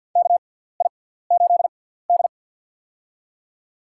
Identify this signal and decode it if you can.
Morse code, heard as a keyed tone, about 47 wpm, spaced as for 17 wpm, 700 Hz. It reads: KI8D
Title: USA Call Signs - 4 Characters - 6x Character Spacing - 50wpm